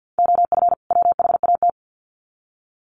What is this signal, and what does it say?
Morse code, heard as a keyed tone, about 50 wpm, 700 Hz. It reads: OF WHAT
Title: Sets of 2 Words 50wpm